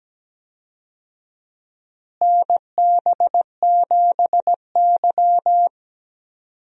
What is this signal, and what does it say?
Morse code, heard as a keyed tone, about 17 wpm, 700 Hz. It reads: NB7Y
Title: USA Call Signs - 4 Characters - Repeated 2x - 17wpm